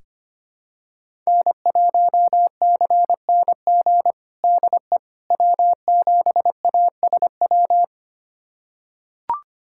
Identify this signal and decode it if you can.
Morse code, heard as a keyed tone, about 25 wpm, 700 Hz. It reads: N1CNG DE W7ASW
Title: USA Callsign DE Callsign - 5 Characters - 25wpm